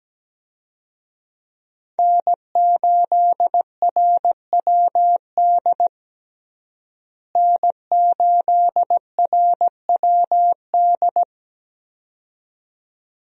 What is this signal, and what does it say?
Morse code, heard as a keyed tone, about 17 wpm, 700 Hz. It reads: N8RWD N8RWD
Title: USA Call Signs - 5 Characters - Repeated 2x - 17wpm